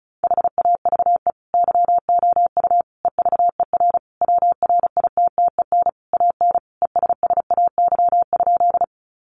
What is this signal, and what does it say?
Morse code, heard as a keyed tone, about 34 wpm, 700 Hz. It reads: HAVE YOU EVER WRITTEN AN ESSAY?
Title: Sentences from Top 1500 Words 35wpm